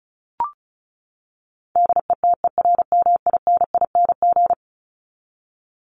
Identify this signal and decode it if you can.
Morse code, heard as a keyed tone, about 34 wpm, 700 Hz. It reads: DETERMINING
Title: ING Suffix - Top 100 Words with Base Words 35wpm